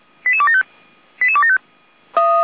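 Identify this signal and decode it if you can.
Five-tone selective call (ZVEI), heard as quick runs of five tones, tones of about 70 ms. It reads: 88275 88275
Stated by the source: ZVEI five-tone selective call received on SDR, Feuerwehr Orlamuende.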